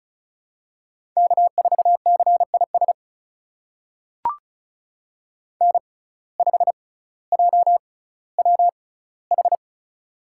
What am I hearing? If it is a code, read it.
Morse code, heard as a keyed tone, about 36 wpm, spaced as for 18 wpm, 700 Hz. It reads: K4CIS N5JWH
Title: USA Call Signs - 5 Characters - 6x Character Spacing - 35wpm